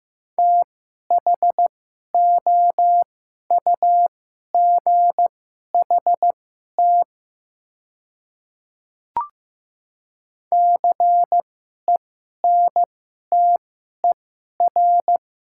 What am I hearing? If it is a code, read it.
Morse code, heard as a keyed tone, about 15 wpm, spaced as for 11 wpm, 690 Hz. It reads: THOUGHT CENTER
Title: 2x Character Spacing - Top 500 Words - 15wpm